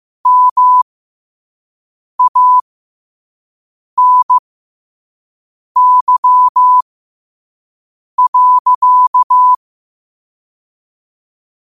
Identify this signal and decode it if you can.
Morse code, heard as a keyed tone, about 15 wpm, spaced as for 5 wpm, 1000 Hz. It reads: MANY.
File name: qotdfiles / 5wpm